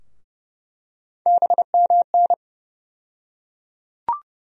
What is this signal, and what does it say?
Morse code, heard as a keyed tone, about 30 wpm, 700 Hz. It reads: BMN
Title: USA Call Sign Postfixes - 3 Letters 30wpm